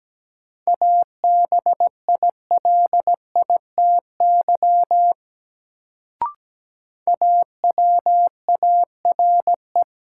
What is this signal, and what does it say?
Morse code, heard as a keyed tone, about 17 wpm, 700 Hz. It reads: ABILITY AWARE
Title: Sentences from Top 1200 Words - New Word Drill Alt - 17wpm